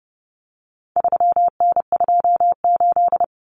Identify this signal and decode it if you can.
Morse code, heard as a keyed tone, about 30 wpm, 700 Hz. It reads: 3N28
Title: Common Cut Numbers - 4-Digit 30wpm